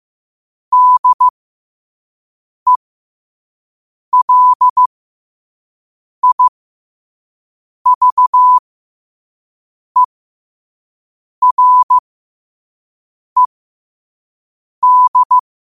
Morse code, heard as a keyed tone, about 15 wpm, spaced as for 5 wpm, 1000 Hz. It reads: DELIVERED
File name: qotdfiles / 5wpm